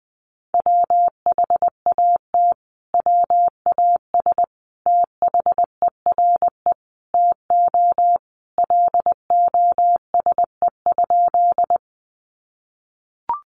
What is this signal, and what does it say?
Morse code, heard as a keyed tone, about 20 wpm, 700 Hz. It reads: WHAT WAS THERE TO LOSE?
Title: Sentences from Top 1100 Words 20wpm